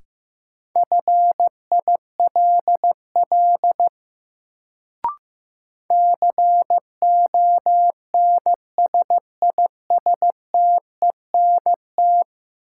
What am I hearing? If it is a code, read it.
Morse code, heard as a keyed tone, about 15 wpm, 700 Hz. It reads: FILL CONSISTENT
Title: Sentences from Top 1500 Words - New Word Drill Alt - 15wpm